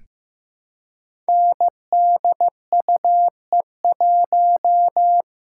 Morse code, heard as a keyed tone, about 15 wpm, 700 Hz. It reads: NDUE1
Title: Code Groups - 5 Letter:Numbers - 15wpm